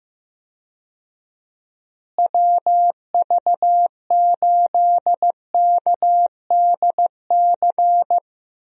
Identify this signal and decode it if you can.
Morse code, heard as a keyed tone, about 15 wpm, 700 Hz. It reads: WV8KDC
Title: USA Call Signs - 6 Characters - Repeated 2x - 15wpm